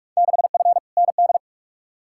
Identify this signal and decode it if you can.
Morse code, heard as a keyed tone, about 42 wpm, 700 Hz. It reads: 6F ND
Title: Field Day Exchanges Encore - 45wpm